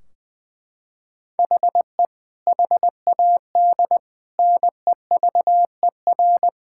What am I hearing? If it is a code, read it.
Morse code, heard as a keyed tone, about 21 wpm, 700 Hz. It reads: HE HAD NEVER
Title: Sentences from Top 1600 Words 20wpm